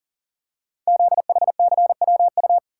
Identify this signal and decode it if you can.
Morse code, heard as a keyed tone, about 40 wpm, 700 Hz. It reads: ZHCWU
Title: Code Groups - 5 Letters - 40wpm